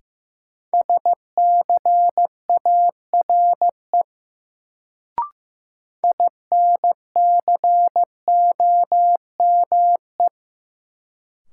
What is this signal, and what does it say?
Morse code, heard as a keyed tone, about 15 wpm, 700 Hz. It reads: SCARE INCOME
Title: Sentences from Top 1000 Words - Review Part 5 - New Word Drill - 15wpm